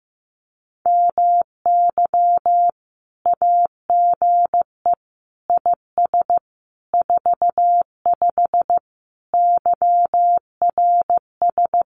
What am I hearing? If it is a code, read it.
Morse code, heard as a keyed tone, about 15 wpm, 700 Hz. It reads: MY AGE IS 45 YRS
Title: Common QSO Phrases Encore 15wpm